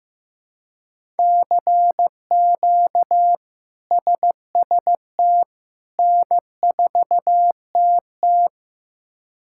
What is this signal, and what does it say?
Morse code, heard as a keyed tone, about 15 wpm, 700 Hz. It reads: CQ SST N4TT
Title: K1USN SST QSO Practice - 15wpm